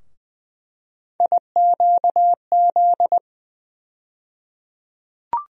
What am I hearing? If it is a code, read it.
Morse code, heard as a keyed tone, about 20 wpm, 700 Hz. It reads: IQZ
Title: USA Call Sign Postfixes - 3 Letters 20wpm